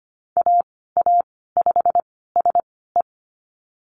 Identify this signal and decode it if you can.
Morse code, heard as a keyed tone, about 25 wpm, spaced as for 16 wpm, 690 Hz. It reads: AA5SE
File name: Lesson 15g - Number 5 - Call Signs 25wpm